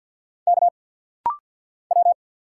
Morse code, heard as a keyed tone, about 50 wpm, 700 Hz. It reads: K W
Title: Single Letter-Number - Rapid-Fire 50wpm